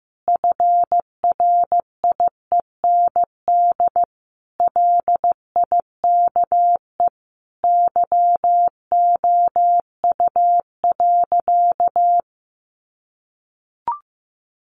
Morse code, heard as a keyed tone, about 15 wpm, 700 Hz. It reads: FRIEND LIKE YOU.
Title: Sentences from Top 1100 Words 15wpm